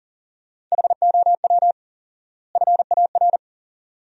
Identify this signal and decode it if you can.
Morse code, heard as a keyed tone, about 40 wpm, 700 Hz. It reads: HOW FAR
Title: 4x Word-Spacing - Sentences from Top 400 Words - 40wpm